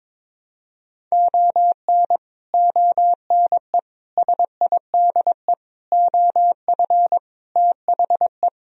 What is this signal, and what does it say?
Morse code, heard as a keyed tone, about 23 wpm, 700 Hz. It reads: ON ONE SIDE OF THE
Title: Speed-Racing - Sets of 5 Words 22wpm